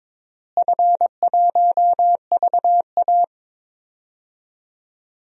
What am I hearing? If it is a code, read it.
Morse code, heard as a keyed tone, about 22 wpm, 700 Hz. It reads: F1VA
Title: DXCC-100 Call Signs - 4 Characters - Repeated 3x - 22wpm